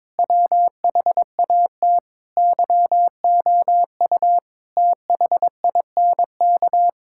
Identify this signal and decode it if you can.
Morse code, heard as a keyed tone, about 22 wpm, 700 Hz. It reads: WHAT YOU THINK
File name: practice / 22wpm / Sets of 8 Words - 22wpm